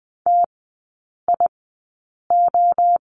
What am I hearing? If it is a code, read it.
Morse code, heard as a keyed tone, about 20 wpm, spaced as for 8 wpm, 690 Hz. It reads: TIO
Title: 060 Lesson 11e - Letter h - Three Characters 20wpm